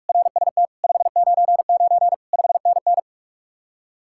Morse code, heard as a keyed tone, about 49 wpm, 700 Hz. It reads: RST 599 5NN